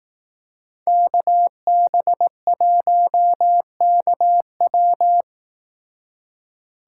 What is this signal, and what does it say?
Morse code, heard as a keyed tone, about 18 wpm, 700 Hz. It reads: KB1KW
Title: Speed Racing - USA Call Signs 15wpm